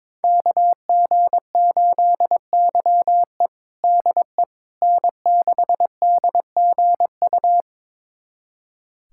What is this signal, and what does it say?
Morse code, heard as a keyed tone, about 22 wpm, 700 Hz. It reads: KG8YE DE N6DGU
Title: USA Callsign DE Callsign - 5 Characters - 22wpm